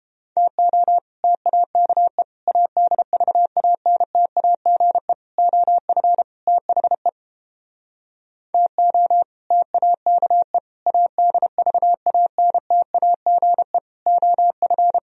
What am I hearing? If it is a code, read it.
Morse code, heard as a keyed tone, about 29 wpm, 700 Hz. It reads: TO TAKE ADVANTAGE OF THE TO TAKE ADVANTAGE OF